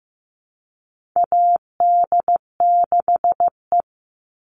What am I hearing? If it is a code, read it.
Morse code, heard as a keyed tone, about 15 wpm, 700 Hz. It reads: AD6E
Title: Call Signs Repeated 2x - 15wpm